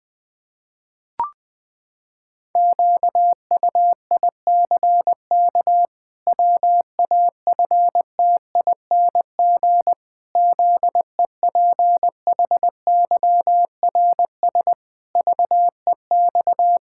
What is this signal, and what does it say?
Morse code, heard as a keyed tone, about 20 wpm, 700 Hz. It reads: QUICK WAFTING ZEPHYRS VEX
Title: Pangram Challenge v2 - 20wpm